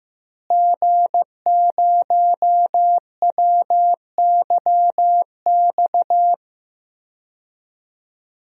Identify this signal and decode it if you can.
Morse code, heard as a keyed tone, about 15 wpm, 700 Hz. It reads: G0WYX
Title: DXCC-100 Call Signs - 5 Characters - Repeated 2x - 15wpm